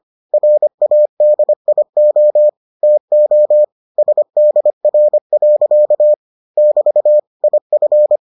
Morse code, heard as a keyed tone, about 25 wpm, 600 Hz. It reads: RADIO TO SDR. <BT> IF